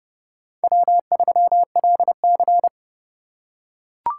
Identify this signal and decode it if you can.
Morse code, heard as a keyed tone, about 32 wpm, 700 Hz. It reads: W3LC
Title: USA Call Signs - 4 Characters - 3x Character Spacing - 30wpm